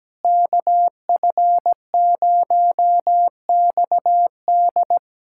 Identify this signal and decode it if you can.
Morse code, heard as a keyed tone, about 17 wpm, 700 Hz. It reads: KF0XD
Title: USA Callsign DE Callsign - 5 Characters - 17wpm